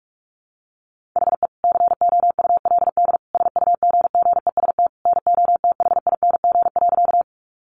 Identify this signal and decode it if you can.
Morse code, heard as a keyed tone, about 44 wpm, 700 Hz. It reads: HE COULD SUGGEST NOTHING.
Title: Sentences from Top 1300 Words 45wpm